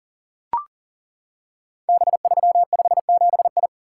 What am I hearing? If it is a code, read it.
Morse code, heard as a keyed tone, about 39 wpm, 700 Hz. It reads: B357I
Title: Code Groups - 5 Letter:Numbers - 40wpm